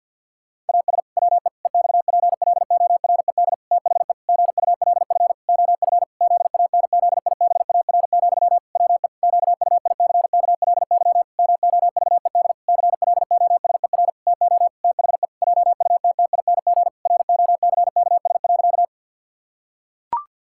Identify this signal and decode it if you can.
Morse code, heard as a keyed tone, about 50 wpm, 700 Hz. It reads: AS WE EXPLORED THE GULF OF ZANZIBAR, WE QUICKLY MOVED CLOSER TO THE JUTTING ROCKS.
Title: Pangram Challenge v2 - 50wpm